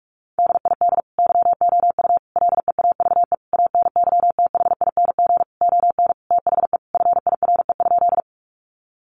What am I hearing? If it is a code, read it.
Morse code, heard as a keyed tone, about 44 wpm, 700 Hz. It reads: DID YOU LEAVE ANYTHING ON THE FIRE?
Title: Sentences from Top 1000 Words 45wpm